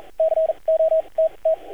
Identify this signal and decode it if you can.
Morse code, heard as a keyed tone, about 42 wpm, spaced as for 30 wpm, 640 Hz, with radio noise. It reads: COTT
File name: archived / CRSnet